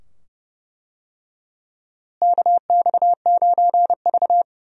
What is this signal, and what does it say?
Morse code, heard as a keyed tone, about 31 wpm, 700 Hz. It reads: KX9V
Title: USA Call Signs - 4 Characters - Repeated 2x - 30wpm